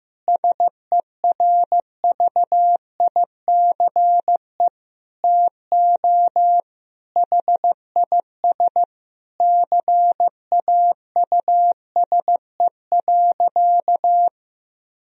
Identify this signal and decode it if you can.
Morse code, heard as a keyed tone, about 16 wpm, 700 Hz. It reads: SERVICE TO HIS CAUSE.
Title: Sentences from Top 1000 Words - Review Part 4 - 15wpm